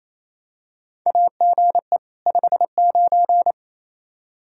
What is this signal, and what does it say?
Morse code, heard as a keyed tone, about 28 wpm, 700 Hz. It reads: AGE 59
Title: K1USN SST QSO Practice 28wpm